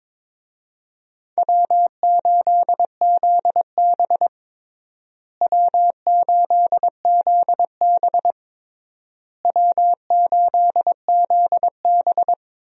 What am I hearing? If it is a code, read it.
Morse code, heard as a keyed tone, about 22 wpm, 700 Hz. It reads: W8ZB W8ZB W8ZB
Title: USA Call Signs - 4 Characters - Repeated 3x - 22wpm